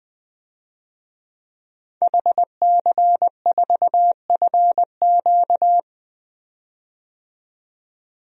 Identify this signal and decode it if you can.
Morse code, heard as a keyed tone, about 21 wpm, 700 Hz. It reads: HC4FQ
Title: DXCC-100 Call Signs - 5 Characters - 20wpm